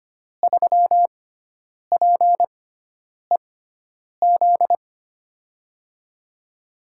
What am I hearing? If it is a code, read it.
Morse code, heard as a keyed tone, about 25 wpm, spaced as for 9 wpm, 700 Hz. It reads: 3PEZ
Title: USA Call Signs - 5 Characters - 6x Character Spacing - 25wpm